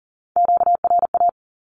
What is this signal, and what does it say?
Morse code, heard as a keyed tone, about 40 wpm, 700 Hz. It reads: QRA
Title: Q-Codes 40wpm